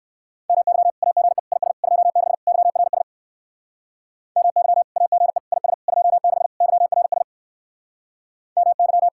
Morse code, heard as a keyed tone, about 59 wpm, 700 Hz. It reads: MY AGE IS 26 YRS MY AGE IS 26 YRS MY